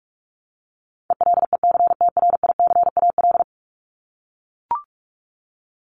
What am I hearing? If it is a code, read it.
Morse code, heard as a keyed tone, about 44 wpm, 700 Hz. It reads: ELECTRICAL
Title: Top 751-1000 Words 45wpm